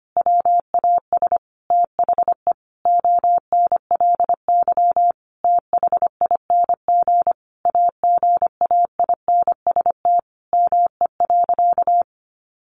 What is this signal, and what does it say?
Morse code, heard as a keyed tone, about 25 wpm, 700 Hz. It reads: WAS THE ONLY THING AGAINST ME.
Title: Sentences from Top 1100 Words 25wpm